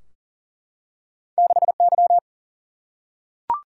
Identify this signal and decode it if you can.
Morse code, heard as a keyed tone, about 39 wpm, 700 Hz. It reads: 6Y
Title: Single-Letter Single-Number 40wpm